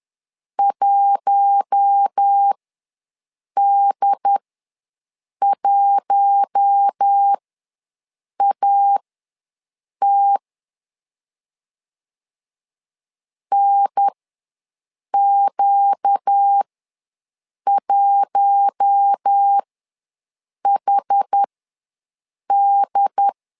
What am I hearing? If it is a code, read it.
Morse code, heard as a keyed tone, about 10 wpm, spaced as for 6 wpm, 800 Hz. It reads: 1D1AT NQ1HD
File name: LEZIONI / 4 Q1 ws 10wpm 2538